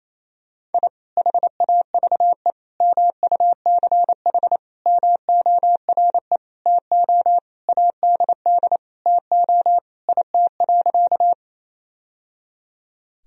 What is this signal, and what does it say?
Morse code, heard as a keyed tone, about 28 wpm, 700 Hz. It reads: I HAVE MUCH MORE TO ADD TO IT.
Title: Sentences from Top 1000 Words - Review Part 1 - 28wpm